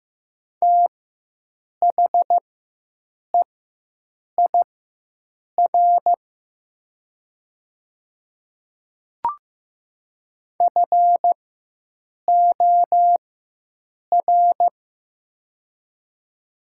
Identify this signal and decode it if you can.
Morse code, heard as a keyed tone, about 15 wpm, spaced as for 7 wpm, 700 Hz. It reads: THEIR FOR
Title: 4x Character Spacing - Top 200 Words - 15wpm